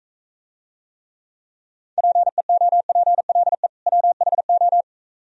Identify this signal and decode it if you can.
Morse code, heard as a keyed tone, about 45 wpm, 700 Hz. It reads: PEOPLE WHO